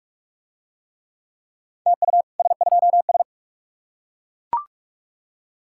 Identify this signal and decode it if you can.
Morse code, heard as a keyed tone, about 39 wpm, 700 Hz. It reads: TU S2S